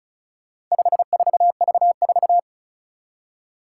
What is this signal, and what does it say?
Morse code, heard as a keyed tone, about 36 wpm, 700 Hz. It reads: 54V4
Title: Four Challenging Letter-Number Combinations 35wpm